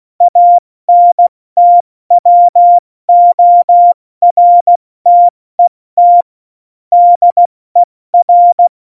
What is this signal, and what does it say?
Morse code, heard as a keyed tone, about 16 wpm, spaced as for 14 wpm, 700 Hz. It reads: ANTWORTET DER
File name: Morsekurs / Witze / Witz 10.Dezember 2019, 70 F80